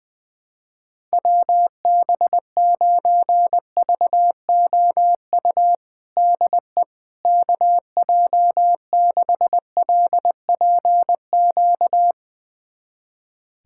Morse code, heard as a keyed tone, about 20 wpm, 700 Hz. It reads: WB9VOU DE KJ6LPQ
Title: USA Callsign DE Callsign - 6 Characters - 20wpm